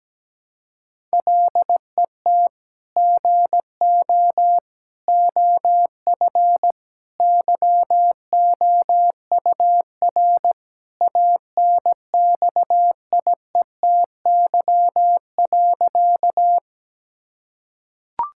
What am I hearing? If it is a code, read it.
Morse code, heard as a keyed tone, about 17 wpm, 700 Hz. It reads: LET GO OF YOUR ANXIETY.
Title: Sentences from Top 1600 Words 17wpm